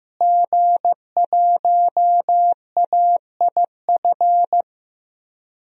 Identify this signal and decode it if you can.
Morse code, heard as a keyed tone, about 15 wpm, 700 Hz. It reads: G1AIF
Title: DXCC-100 Call Signs - 5 Characters - Repeated 3x - 15wpm